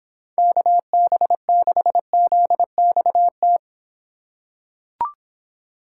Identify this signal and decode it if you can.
Morse code, heard as a keyed tone, about 26 wpm, 700 Hz. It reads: KB6ZXT